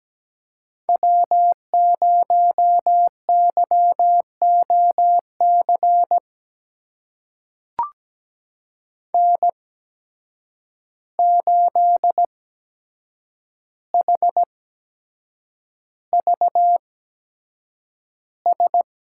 Morse code, heard as a keyed tone, about 17 wpm, spaced as for 7 wpm, 700 Hz. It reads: W0YOC N8HVS
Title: USA Call Signs - 5 Characters - 8x Character Spacing - 17wpm